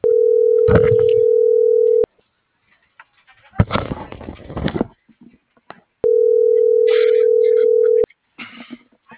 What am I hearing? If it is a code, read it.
Ringback tone (North America), 440 + 480 Hz.